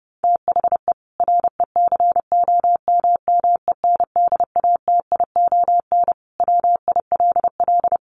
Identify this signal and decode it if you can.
Morse code, heard as a keyed tone, about 30 wpm, 700 Hz. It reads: THE RECOMMENDATION WILL
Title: Sentences from Top 1400 Words 30wpm